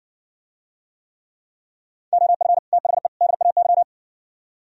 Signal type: Morse code

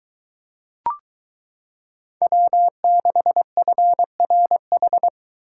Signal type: Morse code

